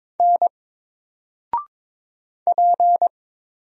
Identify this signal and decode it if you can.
Morse code, heard as a keyed tone, about 22 wpm, 700 Hz. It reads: N P
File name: Single-Letter-Number-22wpm